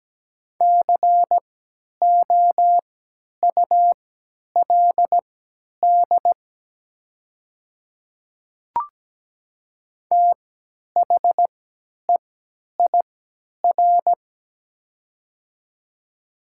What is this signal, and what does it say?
Morse code, heard as a keyed tone, about 17 wpm, spaced as for 10 wpm, 700 Hz. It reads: COULD THEIR
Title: 3x Character Spacing - Top 200 Words - 17wpm